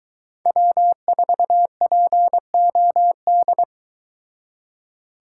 Morse code, heard as a keyed tone, about 22 wpm, 700 Hz. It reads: W4POD